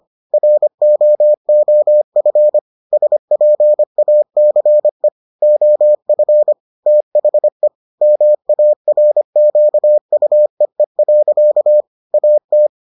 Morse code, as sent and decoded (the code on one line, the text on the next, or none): .-. --- --- ..-. / ... .--. .- -.-. / --- ..-. / - .... / -- .- .-. --.- ..- . .-.-.- / .- -
ROOF SPACE OF THE MARQUEE. AT